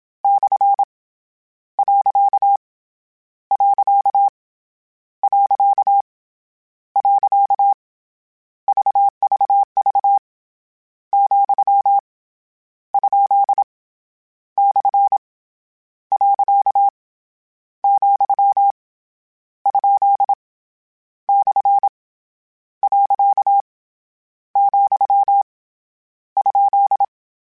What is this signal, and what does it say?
Morse code, heard as a keyed tone, about 27 wpm, 800 Hz. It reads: / . . VVV , ? / . , ? / . , ?
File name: Misc / Misc-mp3s / Punct @25